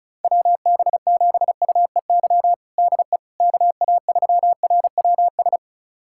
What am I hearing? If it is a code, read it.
Morse code, heard as a keyed tone, about 36 wpm, 700 Hz. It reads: WB7UEY DE KA3RWS
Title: USA Callsign DE Callsign - 6 Characters - 35wpm